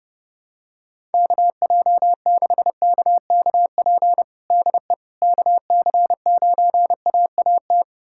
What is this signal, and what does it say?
Morse code, heard as a keyed tone, about 30 wpm, 700 Hz. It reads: KJ6KKP DE KC9AAT
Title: USA Callsign DE Callsign - 6 Characters - 30wpm